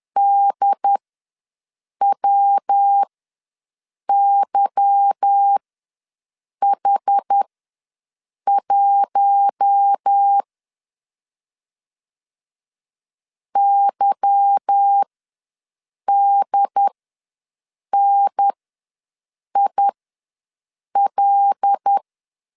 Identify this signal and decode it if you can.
Morse code, heard as a keyed tone, about 10 wpm, spaced as for 6 wpm, 800 Hz. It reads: DWYH1 YDNIL
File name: LEZIONI / 8 YJ45 / 8 Y ws 10wpm 3952